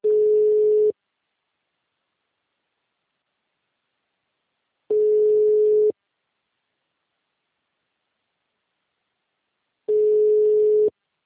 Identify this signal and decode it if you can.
Ringback tone (Europe, 425 Hz).
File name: out- -2033- - - .48579